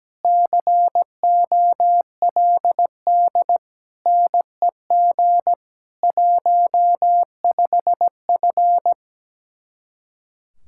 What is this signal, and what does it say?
Morse code, heard as a keyed tone, about 18 wpm, 700 Hz. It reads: COLD NEG 15F
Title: Speed-Racing Common QSO Phrases - 17wpm